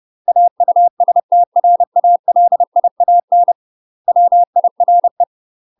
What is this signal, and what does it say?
Morse code, heard as a keyed tone, about 30 wpm, 700 Hz. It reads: AUSTRALIAN WIRE
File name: UK MASTER QSOs TRAINING / 30WPM / UK MASTER QSOs 0018